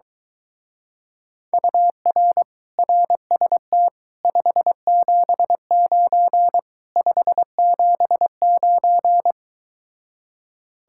Morse code, heard as a keyed tone, about 24 wpm, 700 Hz. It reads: UR RST 579 579